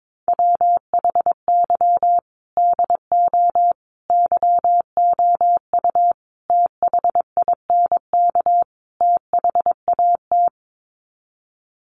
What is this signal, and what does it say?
Morse code, heard as a keyed tone, about 22 wpm, 700 Hz. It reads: WHY DO YOU THINK THAT
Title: Sets of 5 Words Encore2 22wpm